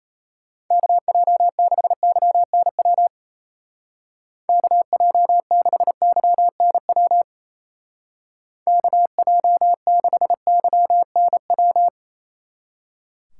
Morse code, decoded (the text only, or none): KJ6YNW KJ6YNW KJ6YNW